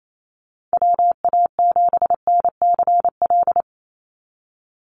Morse code, heard as a keyed tone, about 28 wpm, 700 Hz. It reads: WA7NCL
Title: NA SOTA Activator Callsigns 28wpm